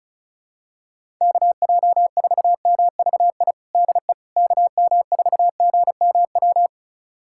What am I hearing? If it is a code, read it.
Morse code, heard as a keyed tone, about 36 wpm, 700 Hz. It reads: KJ4MVI DE KM4GMW
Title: USA Callsign DE Callsign - 6 Characters - 35wpm